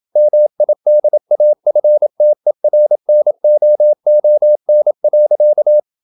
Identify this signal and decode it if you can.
Morse code, heard as a keyed tone, about 27 wpm, 600 Hz. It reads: MIDAFTERNOON.
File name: Never 0027wpm